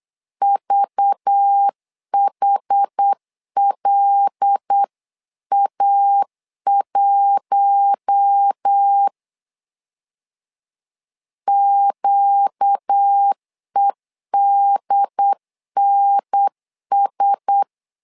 Morse code, heard as a keyed tone, about 8 wpm, 800 Hz. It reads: VHLA1 QEDNS
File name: LEZIONI / 5 LBVC / 5 LBVC 8wpm 2935